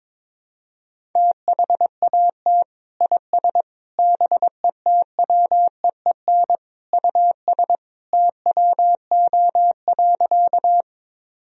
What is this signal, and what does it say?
Morse code, heard as a keyed tone, about 22 wpm, 700 Hz. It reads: THAT IS BETWEEN US TWO.
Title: Sentences from Top 1000 Words - Review Part 1 - 22wpm